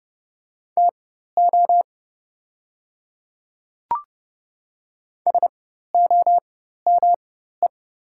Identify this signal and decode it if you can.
Morse code, heard as a keyed tone, about 30 wpm, spaced as for 14 wpm, 690 Hz. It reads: TO SOME
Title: 4x Character Spacing - Top 100 Words - 30wpm